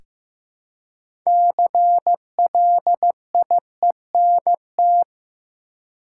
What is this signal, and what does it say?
Morse code, heard as a keyed tone, about 15 wpm, 700 Hz. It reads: CLIENT
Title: Six Letter Words - 15wpm